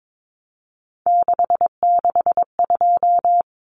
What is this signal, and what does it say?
Morse code, heard as a keyed tone, about 22 wpm, 700 Hz. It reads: 662
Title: Numbers - 3-Digit 22wpm